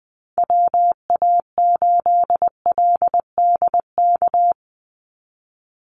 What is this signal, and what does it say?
Morse code, heard as a keyed tone, about 20 wpm, 700 Hz. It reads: WA8LDK
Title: USA Call Signs 20wpm